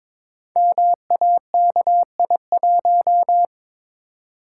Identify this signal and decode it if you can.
Morse code, heard as a keyed tone, about 23 wpm, 700 Hz. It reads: MAKI1
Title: Code Groups - 5 Letter:Numbers - 22wpm